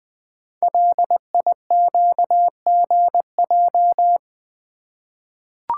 Morse code, heard as a keyed tone, about 20 wpm, 700 Hz. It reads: LIQGJ
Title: Code Groups - 5 Letters - 20wpm